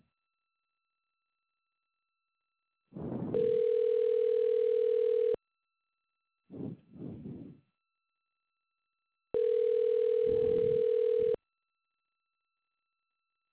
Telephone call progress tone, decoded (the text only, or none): ringback tone